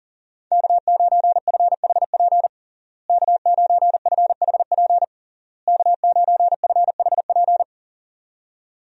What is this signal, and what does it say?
Morse code, heard as a keyed tone, about 40 wpm, 700 Hz. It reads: K9FHP K9FHP K9FHP
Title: USA Call Signs - 5 Characters - Repeated 3x - 40wpm